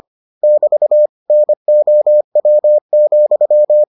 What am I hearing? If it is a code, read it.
Morse code, heard as a keyed tone, about 25 wpm, 600 Hz. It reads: <BT> NOW,